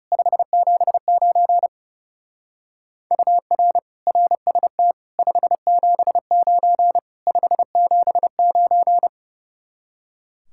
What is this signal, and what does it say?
Morse code, heard as a keyed tone, about 38 wpm, 700 Hz. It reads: 579 UR RST 579 579